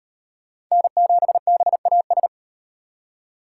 Morse code, heard as a keyed tone, about 32 wpm, 700 Hz. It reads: N7BAS